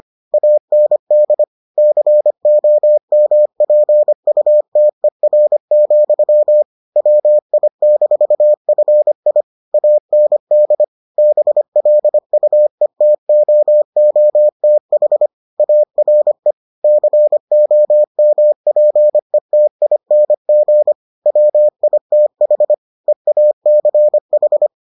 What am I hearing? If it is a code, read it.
Morse code, heard as a keyed tone, about 25 wpm, 600 Hz. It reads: AND COMPUTER, WI-FI AND BLUETOOTH ARE COMPETING WITH EACH